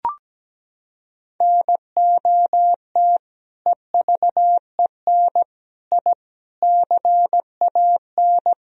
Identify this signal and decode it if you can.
Morse code, heard as a keyed tone, about 17 wpm, 700 Hz. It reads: NOT EVEN I CAN
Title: Sentences from Top 1000 Words - Review Part 2 - 17wpm